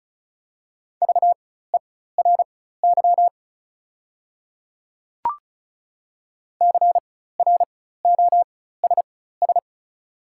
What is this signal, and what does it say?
Morse code, heard as a keyed tone, about 36 wpm, spaced as for 16 wpm, 700 Hz. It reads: VERY CROSS
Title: 4x Character Spacing - Top 500 Words - 35wpm